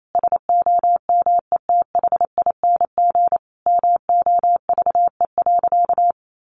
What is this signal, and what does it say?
Morse code, heard as a keyed tone, about 28 wpm, 700 Hz. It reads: SOMETHING MOVE.
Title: Sentences from Top 1200 Words 28wpm